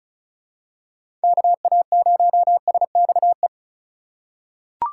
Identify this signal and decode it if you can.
Morse code, heard as a keyed tone, about 36 wpm, 700 Hz. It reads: KA0SXE
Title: USA Call Signs - 6 Characters - 4x Character Spacing - 35wpm